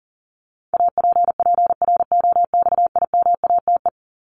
Morse code, heard as a keyed tone, about 40 wpm, 700 Hz. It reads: APPROXIMATE
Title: LY Suffix - Top 100 Words with Base Words 40wpm